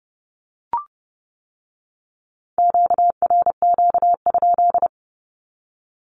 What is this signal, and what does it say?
Morse code, heard as a keyed tone, about 30 wpm, 700 Hz. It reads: QRQ?
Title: Q-Codes with Spoken Meaning - 30wpm